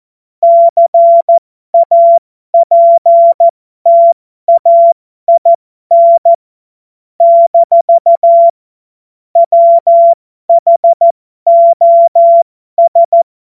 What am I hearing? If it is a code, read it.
Morse code, heard as a keyed tone, about 14 wpm, spaced as for 12 wpm, 680 Hz. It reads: CAPTAIN - WHOS
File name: Morsekurs / Witze